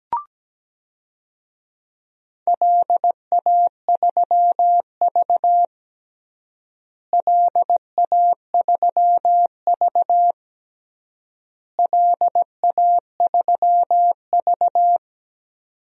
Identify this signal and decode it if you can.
Morse code, heard as a keyed tone, about 17 wpm, 700 Hz. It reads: LA3V LA3V LA3V
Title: DXCC-100 Call Signs - 4 Characters - Repeated 3x - 17wpm